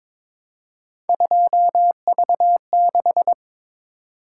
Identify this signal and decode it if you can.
Morse code, heard as a keyed tone, about 22 wpm, 700 Hz. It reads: 2V6
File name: practice / 22wpm / Three Letter-Number Combinations - 22wpm